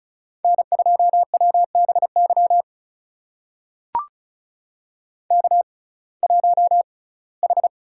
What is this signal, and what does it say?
Morse code, heard as a keyed tone, about 36 wpm, spaced as for 21 wpm, 700 Hz. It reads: N2WBY K1H
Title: USA Call Signs - 5 Characters - 6x Character Spacing - 35wpm